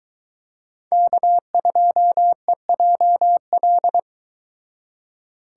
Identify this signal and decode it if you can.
Morse code, heard as a keyed tone, about 24 wpm, 700 Hz. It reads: K2EJL